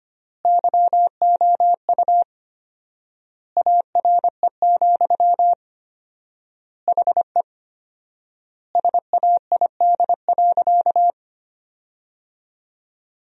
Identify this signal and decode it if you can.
Morse code, heard as a keyed tone, about 25 wpm, 700 Hz. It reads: YOU ARE, HE SAID.
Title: 4x Word-Spacing - Sentences from Top 100 Words - 25wpm